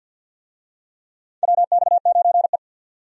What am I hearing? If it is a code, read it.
Morse code, heard as a keyed tone, about 47 wpm, 700 Hz. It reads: WX9E
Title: USA Call Signs - 4 Characters - 50wpm